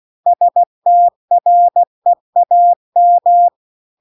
Morse code, heard as a keyed tone, about 16 wpm, 700 Hz. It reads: STREAM
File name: CW-The-Big-Twoheaded-River-2-16WPM